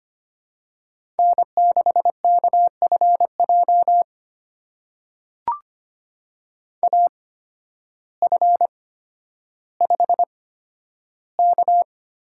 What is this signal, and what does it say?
Morse code, heard as a keyed tone, about 25 wpm, spaced as for 12 wpm, 700 Hz. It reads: N6KFJ AF5K
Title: USA Call Signs - 5 Characters - 8x Character Spacing - 25wpm